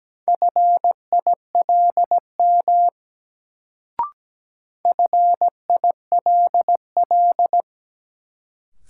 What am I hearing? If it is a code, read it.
Morse code, heard as a keyed tone, about 17 wpm, 700 Hz. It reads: FILM FILL
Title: Sentences from Top 1000 Words - Review Part 3 - New Word Drill - 17wpm